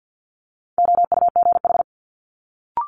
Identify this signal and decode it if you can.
Morse code, heard as a keyed tone, about 50 wpm, 700 Hz. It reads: OUGH
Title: Multi-letter Phonemes Quick Fire 50wpm